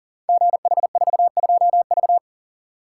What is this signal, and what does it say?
Morse code, heard as a keyed tone, about 38 wpm, 690 Hz. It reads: GH42V
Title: Code Groups - 5 Letter:Numbers - 40wpm